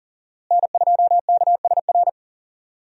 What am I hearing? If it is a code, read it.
Morse code, heard as a keyed tone, about 40 wpm, 700 Hz. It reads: N2KSR
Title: USA Call Signs - 5 Characters - Repeated 3x - 40wpm